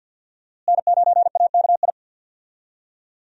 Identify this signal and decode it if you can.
Morse code, heard as a keyed tone, about 51 wpm, 700 Hz. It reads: N9AKI
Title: USA Call Signs - 5 Characters - Repeated 3x - 50wpm